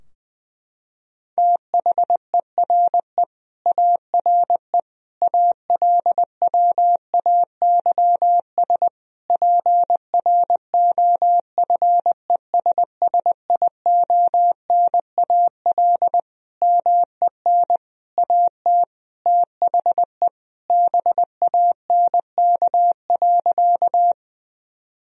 Morse code, heard as a keyed tone, about 20 wpm, 700 Hz. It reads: THERE ARE ALWAYS PROFESSIONAL MEN AT THE BANK.
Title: Sentences from Top 1000 Words - Review Part 4 - 20wpm